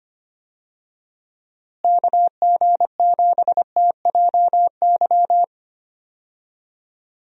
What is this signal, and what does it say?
Morse code, heard as a keyed tone, about 25 wpm, 700 Hz. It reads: KG7TJY
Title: USA Call Signs - 6 Characters - Repeated 3x - 25wpm